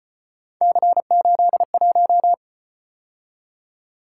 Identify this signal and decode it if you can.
Morse code, heard as a keyed tone, about 33 wpm, 700 Hz. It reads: C81